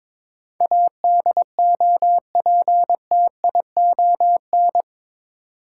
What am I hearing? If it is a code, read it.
Morse code, heard as a keyed tone, about 22 wpm, 700 Hz. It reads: ADOPTION
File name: practice / 22wpm / Eight Letter Words - 22wpm